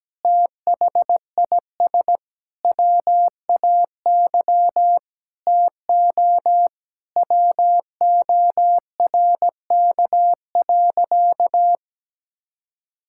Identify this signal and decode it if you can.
Morse code, heard as a keyed tone, about 17 wpm, 700 Hz. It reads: THIS WAY TO WORK.
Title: Sentences from Top 1000 Words - Review Part 5 - 17wpm